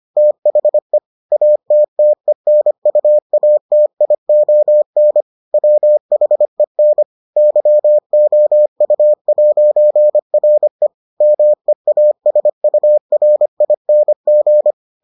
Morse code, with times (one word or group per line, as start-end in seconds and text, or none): 0.2-1.0 THE
1.3-5.2 ATTENUATION
5.5-7.0 WHEN
7.4-10.9 YOU'RE
11.2-14.7 MEASURING